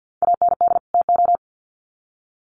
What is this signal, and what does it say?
Morse code, heard as a keyed tone, about 50 wpm, 700 Hz. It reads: AND TO
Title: Sets of 2 Words 50wpm